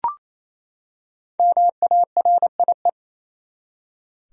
Morse code, heard as a keyed tone, about 27 wpm, 700 Hz. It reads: MARIE
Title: USA Common Names - Max 5 Letters - 28wpm